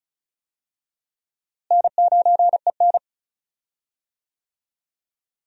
Morse code, heard as a keyed tone, about 36 wpm, 700 Hz. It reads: N9EN
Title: USA Call Signs - 4 Characters - Repeated 3x - 35wpm